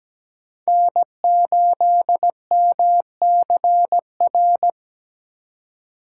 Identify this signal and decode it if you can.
Morse code, heard as a keyed tone, about 17 wpm, 700 Hz. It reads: N8MCR
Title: USA Call Signs - 5 Characters - 4x Character Spacing - 17wpm